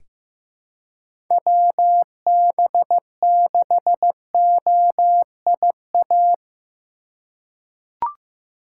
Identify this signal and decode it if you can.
Morse code, heard as a keyed tone, about 15 wpm, 700 Hz. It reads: WB6OIA
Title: USA Call Signs - 6 Characters - 2x Character Spacing - 15wpm